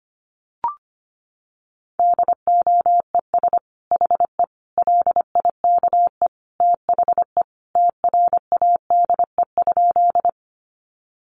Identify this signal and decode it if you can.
Morse code, heard as a keyed tone, about 25 wpm, 700 Hz. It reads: DOES HE LIKE THE TRADE?
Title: Sentences from Top 600 Words 25wpm